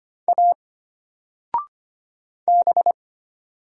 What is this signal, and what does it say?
Morse code, heard as a keyed tone, about 25 wpm, 700 Hz. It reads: A B
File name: Single-Letter-Number-25wpm